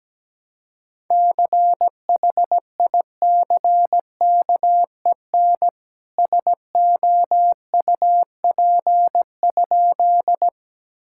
Morse code, heard as a keyed tone, about 17 wpm, 700 Hz. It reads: CHICKEN SOUP?
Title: Sentences from Top 1000 Words - Review Part 6 - 17wpm